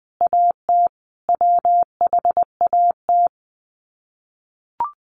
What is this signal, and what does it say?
Morse code, heard as a keyed tone, about 20 wpm, 700 Hz. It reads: AT WHAT
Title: Sets of 5 Words Encore2 20wpm